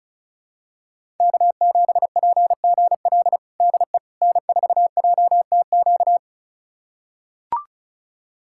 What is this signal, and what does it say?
Morse code, heard as a keyed tone, about 36 wpm, 700 Hz. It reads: K7PGL DE N4JTQ
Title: USA Callsign DE Callsign - 5 Characters - 35wpm